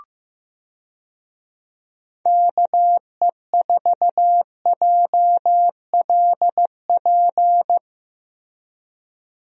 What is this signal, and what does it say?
Morse code, heard as a keyed tone, about 15 wpm, 700 Hz. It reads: KE4JLP
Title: USA Call Signs - 6 Characters - Repeated 3x - 15wpm